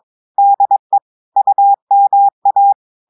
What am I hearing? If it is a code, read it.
Morse code, heard as a keyed tone, about 22 wpm, 800 Hz. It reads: DE UMA
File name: audios / portugais / portugais texte en clair 1 vitesse 22 mots minutes